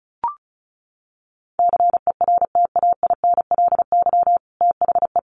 Morse code, heard as a keyed tone, about 34 wpm, 700 Hz. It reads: CERTAINLY THE
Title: Sentences from Top 1200 Words 35wpm